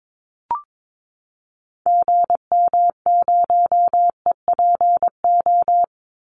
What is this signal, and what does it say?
Morse code, heard as a keyed tone, about 22 wpm, 700 Hz. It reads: GM0EPO
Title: Call Signs - 22wpm